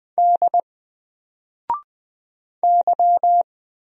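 Morse code, heard as a keyed tone, about 20 wpm, 700 Hz. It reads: D Y
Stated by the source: Single letter and number Morse drill at 20 wpm, from N4LCD.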